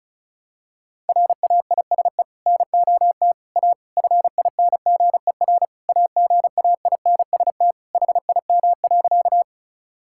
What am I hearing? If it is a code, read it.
Morse code, heard as a keyed tone, about 35 wpm, 700 Hz. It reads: RAISE NOT A FINGER AGAINST HIM.
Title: Sentences from Top 1600 Words 35wpm